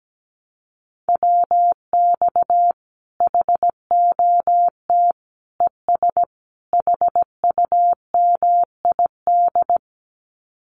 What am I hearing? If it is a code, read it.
Morse code, heard as a keyed tone, about 17 wpm, 700 Hz. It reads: WX HOT ES HUMID
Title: Common QSO Phrases 17wpm